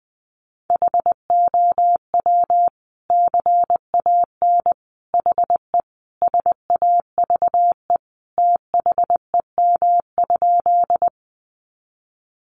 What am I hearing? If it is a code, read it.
Morse code, heard as a keyed tone, about 20 wpm, 700 Hz. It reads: HOW CAN HE SAVE THEM?
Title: Sentences from Top 1100 Words 20wpm